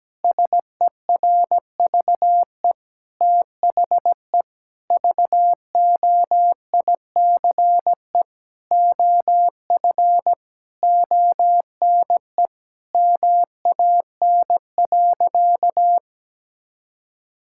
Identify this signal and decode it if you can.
Morse code, heard as a keyed tone, about 17 wpm, 700 Hz. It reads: SERVE THE VOICE OF ONE MAN.
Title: Sentences from Top 1000 Words - Review Part 6 - 17wpm